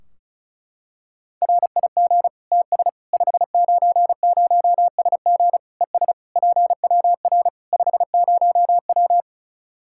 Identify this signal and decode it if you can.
Morse code, heard as a keyed tone, about 34 wpm, 700 Hz. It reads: RIG TS 590SG ES PWR 50W
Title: K1USN SST QSO Practice 35wpm